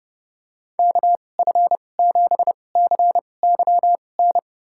Morse code, heard as a keyed tone, about 31 wpm, spaced as for 22 wpm, 700 Hz. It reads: KF7CYN
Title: USA Call Signs - 6 Characters - 2x Character Spacing - 30wpm